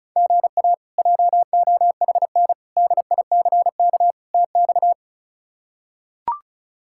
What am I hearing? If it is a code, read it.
Morse code, heard as a keyed tone, about 36 wpm, 700 Hz. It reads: GA JOHN DICK TX
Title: K1USN SST QSO Practice - 35wpm